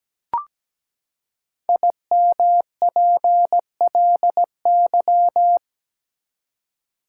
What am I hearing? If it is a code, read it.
Morse code, heard as a keyed tone, about 17 wpm, 700 Hz. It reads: IMPLY
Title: Five Letter Words - 17wpm